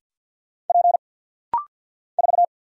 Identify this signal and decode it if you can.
Morse code, heard as a keyed tone, about 50 wpm, 700 Hz. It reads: P 4
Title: Single Letter-Number Alt - Rapid-Fire 50wpm